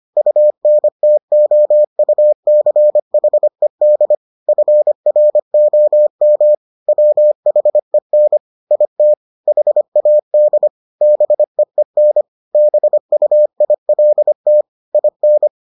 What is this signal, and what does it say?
Morse code, heard as a keyed tone, about 25 wpm, 600 Hz. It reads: UNTOUCHED FROM WHEN IT HAD BEEN BUILT IN